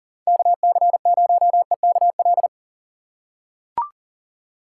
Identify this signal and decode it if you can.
Morse code, heard as a keyed tone, about 40 wpm, 700 Hz. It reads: KC0EKL
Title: USA Call Signs - 6 Characters - 2x Character Spacing - 40wpm